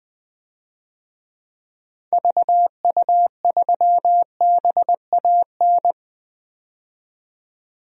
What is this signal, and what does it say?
Morse code, heard as a keyed tone, about 20 wpm, 700 Hz. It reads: VU3BAN
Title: DXCC-100 Call Signs - 6 Characters - 20wpm